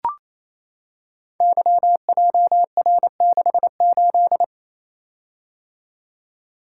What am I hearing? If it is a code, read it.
Morse code, heard as a keyed tone, about 28 wpm, 700 Hz. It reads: YJR68
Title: Code Groups - 5 Letter:Numbers - 28wpm